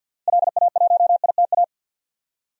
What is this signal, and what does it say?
Morse code, heard as a keyed tone, about 52 wpm, 700 Hz. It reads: LA1ITA
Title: DXCC-100 Call Signs - 6 Characters - Repeated 3x - 50wpm